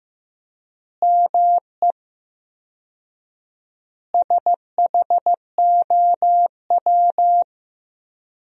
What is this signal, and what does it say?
Morse code, heard as a keyed tone, about 15 wpm, 700 Hz. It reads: ME SHOW
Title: 4x Word-Spacing - Sentences from Top 300 Words - 15wpm